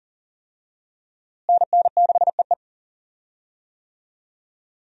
Morse code, heard as a keyed tone, about 40 wpm, 700 Hz. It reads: NN6EE
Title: USA Call Signs - 5 Characters - Repeated 2x - 40wpm